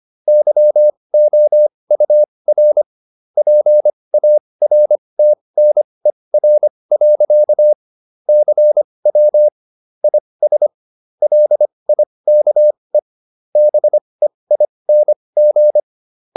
Morse code, heard as a keyed tone, about 25 wpm, spaced as for 20 wpm, 600 Hz. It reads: YOUR PARTNER. CW IS LIKE BEING